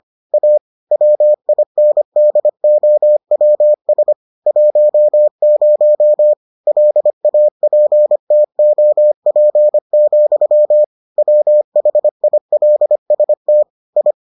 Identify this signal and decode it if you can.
Morse code, heard as a keyed tone, about 25 wpm, 600 Hz. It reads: A WINDOWS 10 LAPTOP, WHILST I